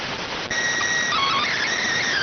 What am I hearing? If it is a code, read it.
SSTV (slow-scan television), PD 290.